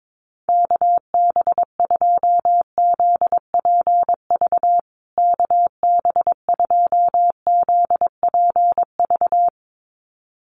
Morse code, heard as a keyed tone, about 22 wpm, 700 Hz. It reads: KB2ZPV KB2ZPV
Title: USA Call Signs Repeated 2x 22wpm